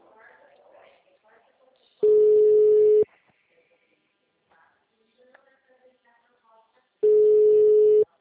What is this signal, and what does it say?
Ringback tone (Europe, 425 Hz), with speech.